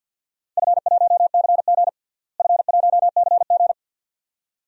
Morse code, heard as a keyed tone, about 51 wpm, 700 Hz. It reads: F1CG F1CG
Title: DXCC-100 Call Signs - 4 Characters - Repeated 2x - 50wpm